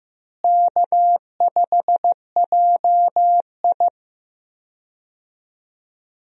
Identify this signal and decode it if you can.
Morse code, heard as a keyed tone, about 15 wpm, 700 Hz. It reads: K5JI
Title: USA Call Signs - 4 Characters - Repeated 3x - 15wpm